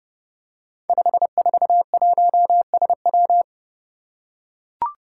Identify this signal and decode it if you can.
Morse code, heard as a keyed tone, about 30 wpm, 700 Hz. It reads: 541SW
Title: Five Challenging Letter-Number Combinations 30wpm